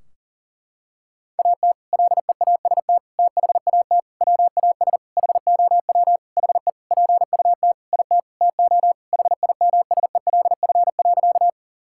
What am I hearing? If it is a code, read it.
Morse code, heard as a keyed tone, about 40 wpm, 700 Hz. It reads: AT LEAST THAT WAS HOW HE PUT IT TO HIMSELF.
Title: Sentences from Top 1600 Words 40wpm